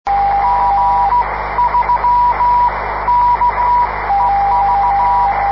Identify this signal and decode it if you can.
Morse code, heard as a keyed tone, about 16 wpm, spaced as for 13 wpm, 970 Hz, with radio noise. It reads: G3KEV